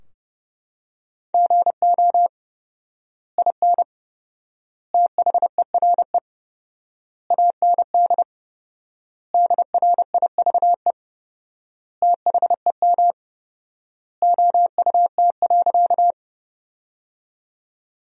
Morse code, heard as a keyed tone, about 30 wpm, 700 Hz. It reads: GO IN THERE AND DRIVE THEM OUT.
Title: 4x Word-Spacing - Sentences from Top 500 Words - 30wpm